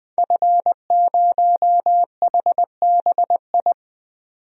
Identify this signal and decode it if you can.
Morse code, heard as a keyed tone, about 20 wpm, 700 Hz. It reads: F0HBI
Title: DXCC-100 Call Signs - 5 Characters - 20wpm